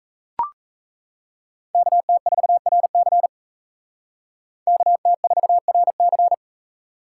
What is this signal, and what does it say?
Morse code, heard as a keyed tone, about 36 wpm, 700 Hz. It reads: KT4RC KT4RC